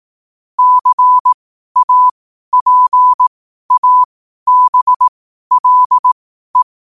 Morse code, heard as a keyed tone, about 18 wpm, spaced as for 12 wpm, 1000 Hz. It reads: CAPABLE
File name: qotdfiles / 12wpm